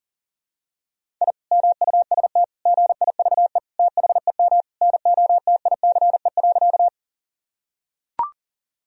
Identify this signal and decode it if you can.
Morse code, heard as a keyed tone, about 40 wpm, 700 Hz. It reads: I MUST GIVE THEM NOTICE.
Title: Sentences from Top 1000 Words - Review Part 2 - 40wpm